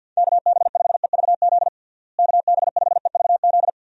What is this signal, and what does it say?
Morse code, heard as a keyed tone, about 49 wpm, 700 Hz. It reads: KB5EVZ KB5EVZ
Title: USA Call Signs - 6 Characters - Repeated 2x - 50wpm